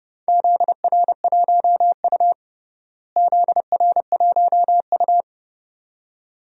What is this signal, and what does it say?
Morse code, heard as a keyed tone, about 32 wpm, 700 Hz. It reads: ZR1U ZR1U
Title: DXCC-100 Call Signs - 4 Characters - Repeated 2x - 30wpm